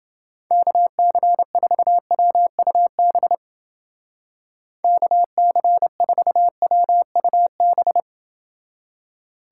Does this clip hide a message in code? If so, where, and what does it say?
Yes: Morse code from 0.5 to 8.0 s, heard as a keyed tone, about 31 wpm, 700 Hz. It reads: KC4WUB KC4WUB